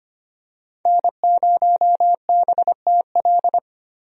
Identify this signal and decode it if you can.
Morse code, heard as a keyed tone, about 25 wpm, 700 Hz. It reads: N0BTL
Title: USA Call Signs - 5 Characters - Repeated 3x - 25wpm